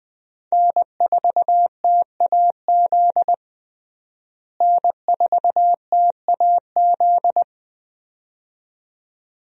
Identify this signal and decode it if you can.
Morse code, heard as a keyed tone, about 20 wpm, 700 Hz. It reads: N4TAZ N4TAZ
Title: USA Call Signs - 5 Characters - Repeated 2x - 20wpm